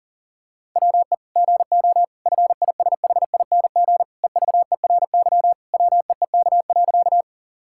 Morse code, heard as a keyed tone, about 37 wpm, 700 Hz. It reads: WE GO FISHING EVERY WEEK.
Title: Sentences from Top 1000 Words - Review Part 5 - 40wpm